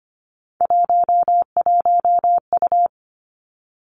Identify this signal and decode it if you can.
Morse code, heard as a keyed tone, about 25 wpm, 700 Hz. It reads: 11U
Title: Three Challenging Letter-Number Combinations 25wpm